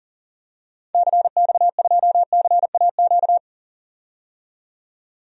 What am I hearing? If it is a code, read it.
Morse code, heard as a keyed tone, about 37 wpm, 700 Hz. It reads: CX2CAQ
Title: DXCC-100 Call Signs - 6 Characters - 40wpm